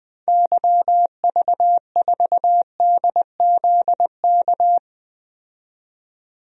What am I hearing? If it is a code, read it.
Morse code, heard as a keyed tone, about 20 wpm, 700 Hz. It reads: YV4DZK
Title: DXCC-100 Call Signs - 6 Characters - 20wpm